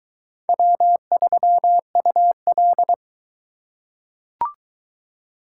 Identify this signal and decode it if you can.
Morse code, heard as a keyed tone, about 21 wpm, 700 Hz. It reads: W3UL